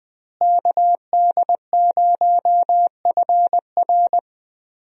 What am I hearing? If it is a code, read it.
Morse code, heard as a keyed tone, about 20 wpm, 700 Hz. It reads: KD0FR
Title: USA Call Signs - 5 Characters - Repeated 3x - 20wpm